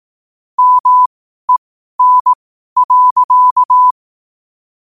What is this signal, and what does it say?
Morse code, heard as a keyed tone, about 18 wpm, spaced as for 12 wpm, 1000 Hz. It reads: MEN.
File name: qotdfiles / 12wpm